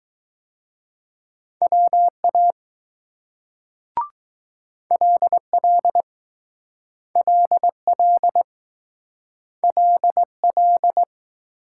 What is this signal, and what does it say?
Morse code, heard as a keyed tone, about 20 wpm, 700 Hz. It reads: WA LL LL LL